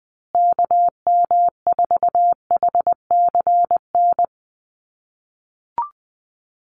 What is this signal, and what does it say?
Morse code, heard as a keyed tone, about 20 wpm, 700 Hz. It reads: KM4HCN
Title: USA Call Signs Repeated 2x 20wpm